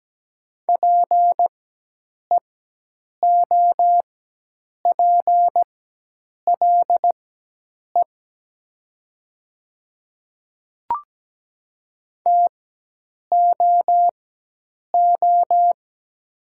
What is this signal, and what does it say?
Morse code, heard as a keyed tone, about 17 wpm, spaced as for 8 wpm, 700 Hz. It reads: PEOPLE TOO
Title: 4x Character Spacing - Top 200 Words - 17wpm